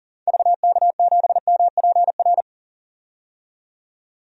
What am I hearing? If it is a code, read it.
Morse code, heard as a keyed tone, about 38 wpm, 700 Hz. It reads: VK7MPR
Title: DXCC-100 Call Signs - 6 Characters - 40wpm